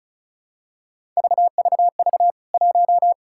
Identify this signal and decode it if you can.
Morse code, heard as a keyed tone, about 36 wpm, 700 Hz. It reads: VVV 1
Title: K1USN SST QSO Practice 35wpm